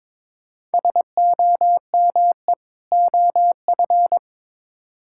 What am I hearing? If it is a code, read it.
Morse code, heard as a keyed tone, about 22 wpm, 700 Hz. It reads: SOME OF
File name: Build A Phrase - 2-Word Phrases - 22wpm